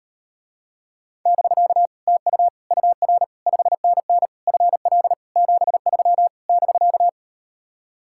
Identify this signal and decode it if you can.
Morse code, heard as a keyed tone, about 38 wpm, 700 Hz. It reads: <BK> TU UR 5NN FL 73 <BK>